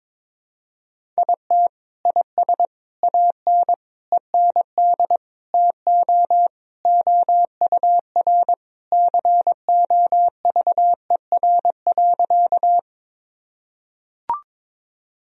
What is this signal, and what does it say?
Morse code, heard as a keyed tone, about 22 wpm, 700 Hz. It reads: IT IS AN END TO OUR COVER.
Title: Sentences from Top 1000 Words - Review Part 1 - 22wpm